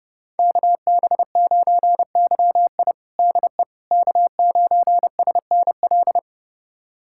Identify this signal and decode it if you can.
Morse code, heard as a keyed tone, about 30 wpm, 700 Hz. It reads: KB9YI DE K9SNL
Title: USA Callsign DE Callsign - 5 Characters - 30wpm